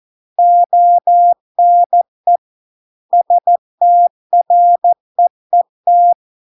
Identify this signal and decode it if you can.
Morse code, heard as a keyed tone, about 14 wpm, 700 Hz. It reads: ONE STREET